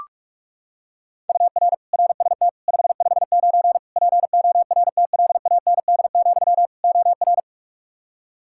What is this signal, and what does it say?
Morse code, heard as a keyed tone, about 41 wpm, 700 Hz. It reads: UR RST 559 PORTLAND, OR
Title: K1USN SST QSO Practice 45wpm